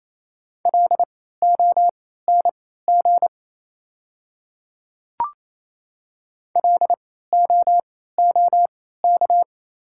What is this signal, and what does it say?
Morse code, heard as a keyed tone, about 27 wpm, spaced as for 16 wpm, 690 Hz. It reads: LONG LOOK
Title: 3x Character Spacing - Top 200 Words - 28wpm